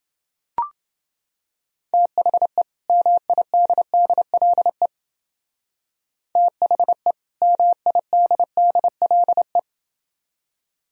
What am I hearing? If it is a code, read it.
Morse code, heard as a keyed tone, about 30 wpm, 700 Hz. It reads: THE MIDDLE THE MIDDLE